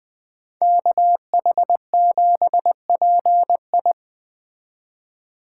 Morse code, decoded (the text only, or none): KH7PI